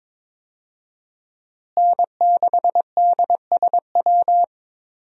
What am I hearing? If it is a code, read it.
Morse code, heard as a keyed tone, about 22 wpm, 700 Hz. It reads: N6DSW
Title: USA Call Signs - 5 Characters - Repeated 3x - 22wpm